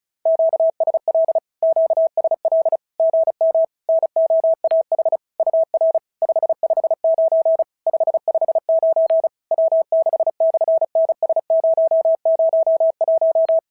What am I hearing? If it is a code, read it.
Morse code, heard as a keyed tone, about 36 wpm, 690 Hz. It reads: QSL QSL GM NOAH UR 559 559 W6/NS001
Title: SOTA QSO Practice 35wpm